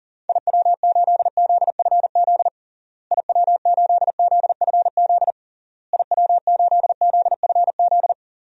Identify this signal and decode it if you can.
Morse code, heard as a keyed tone, about 40 wpm, 700 Hz. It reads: IW8ZFZ IW8ZFZ IW8ZFZ
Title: DXCC-100 Call Signs - 6 Characters - Repeated 3x - 40wpm